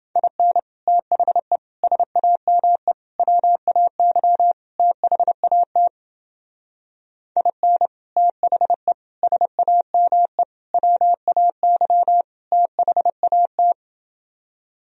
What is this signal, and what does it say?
Morse code, heard as a keyed tone, about 29 wpm, 700 Hz. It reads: IN THE SAME WAY THAT IN THE SAME WAY THAT